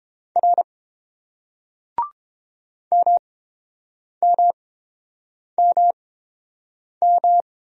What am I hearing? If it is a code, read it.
Morse code, heard as a keyed tone, about 30 wpm, spaced as for 7 wpm, 700 Hz. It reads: R MMMM